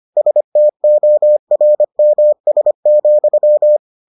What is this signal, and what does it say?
Morse code, heard as a keyed tone, about 25 wpm, 600 Hz. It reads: STORMS,